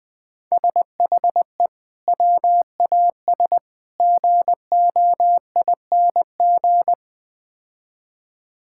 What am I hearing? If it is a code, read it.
Morse code, heard as a keyed tone, about 20 wpm, 700 Hz. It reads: SHE WAS GOING
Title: Build A Phrase - 4-Word Phrases - 20wpm